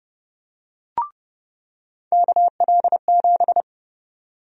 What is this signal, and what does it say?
Morse code, heard as a keyed tone, about 30 wpm, 700 Hz. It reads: KL7
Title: USA Call Sign Prefixes - 30wpm